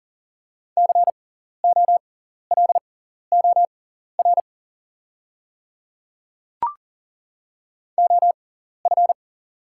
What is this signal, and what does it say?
Morse code, heard as a keyed tone, about 40 wpm, spaced as for 14 wpm, 700 Hz. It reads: COLOR OF
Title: 6x Character Spacing - Top 300 Words - 40wpm